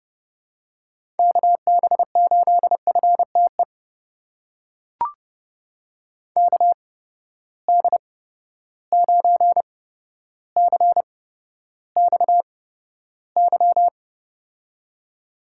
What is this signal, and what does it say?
Morse code, heard as a keyed tone, about 30 wpm, spaced as for 13 wpm, 700 Hz. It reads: KB8FTE KD9CXY
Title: USA Call Signs - 6 Characters - 8x Character Spacing - 30wpm